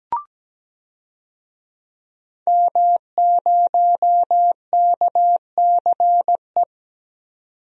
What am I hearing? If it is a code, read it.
Morse code, heard as a keyed tone, about 17 wpm, 700 Hz. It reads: M0KCE
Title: DXCC-100 Call Signs - 5 Characters - 17wpm